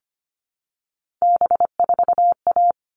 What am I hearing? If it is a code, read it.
Morse code, heard as a keyed tone, about 25 wpm, 700 Hz. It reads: B4A
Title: All Cut Numbers - 3-Digit 25wpm